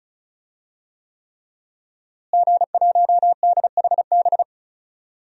Morse code, heard as a keyed tone, about 36 wpm, 700 Hz. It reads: G1DHB
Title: DXCC-100 Call Signs - 5 Characters - Repeated 3x - 35wpm